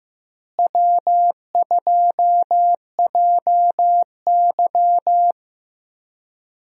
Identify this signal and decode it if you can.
Morse code, heard as a keyed tone, about 15 wpm, 700 Hz. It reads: W2JY
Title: USA Call Signs - 4 Characters - Repeated 3x - 15wpm